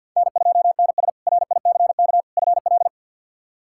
Morse code, heard as a keyed tone, about 50 wpm, 700 Hz. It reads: N2NS RICK FL
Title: K1USN SST QSO Practice - 50wpm